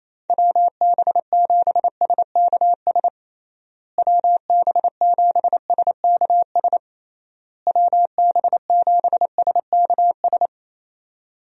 Morse code, heard as a keyed tone, about 28 wpm, 700 Hz. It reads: WB7SKS WB7SKS WB7SKS
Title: USA Call Signs - 6 Characters - Repeated 3x - 28wpm